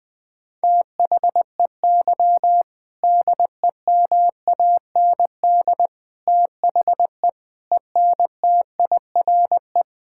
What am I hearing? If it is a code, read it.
Morse code, heard as a keyed tone, about 20 wpm, 700 Hz. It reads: THEY DEMAND THE ENTIRE
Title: Sentences from Top 1000 Words - Review Part 5 - 20wpm